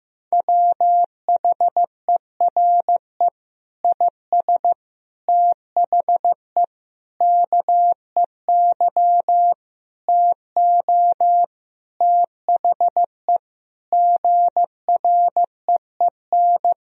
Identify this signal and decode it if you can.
Morse code, heard as a keyed tone, about 15 wpm, 700 Hz. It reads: WHERE IS THE KEY TO THE GREEN
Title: Sentences from Top 1000 Words - Review Part 4 - 15wpm